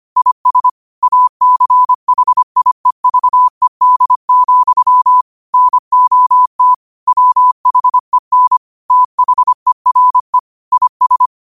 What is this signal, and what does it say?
Morse code, heard as a keyed tone, about 25 wpm, 1000 Hz. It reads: IS ACHIEVED, NOT WHEN THERE IS